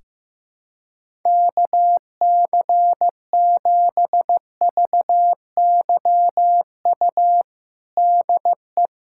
Morse code, heard as a keyed tone, about 15 wpm, 700 Hz. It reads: KC7VYU DE
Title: USA Callsign DE Callsign 15wpm